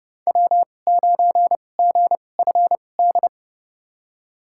Morse code, heard as a keyed tone, about 30 wpm, spaced as for 22 wpm, 700 Hz. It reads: W9GFD
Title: USA Call Signs - 5 Characters - 2x Character Spacing - 30wpm